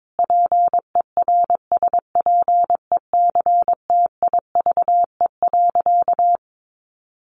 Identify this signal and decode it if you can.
Morse code, heard as a keyed tone, about 22 wpm, 700 Hz. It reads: PERSPECTIVE.
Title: Sentences from Top 1200 Words 22wpm